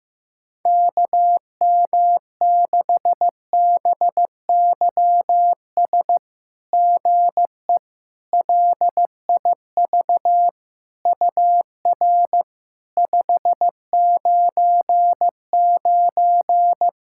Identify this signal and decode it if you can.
Morse code, heard as a keyed tone, about 15 wpm, 690 Hz. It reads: KM6BYS GE LIV UR 599
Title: SOTA QSO Practice 15wpm